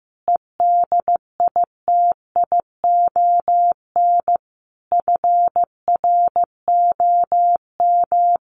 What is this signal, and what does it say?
Morse code, heard as a keyed tone, about 15 wpm, 700 Hz. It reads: EDITION FROM
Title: Sets of 5 Words Encore2 15wpm